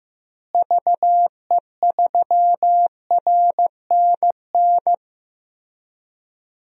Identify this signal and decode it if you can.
Morse code, heard as a keyed tone, about 15 wpm, 700 Hz. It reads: VE3RNN
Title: DXCC-100 Call Signs - 6 Characters - Repeated 2x - 15wpm